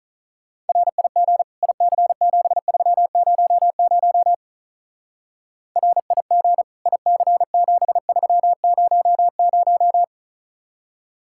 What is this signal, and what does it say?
Morse code, heard as a keyed tone, about 43 wpm, 700 Hz. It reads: RIG IC7300 RIG IC7300
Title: Speed-Racing Common QSO Phrases - 35wpm